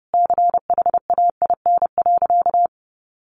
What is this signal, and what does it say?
Morse code, heard as a keyed tone, about 30 wpm, 700 Hz. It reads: CHAIN.
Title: Sentences from Top 1500 Words 30wpm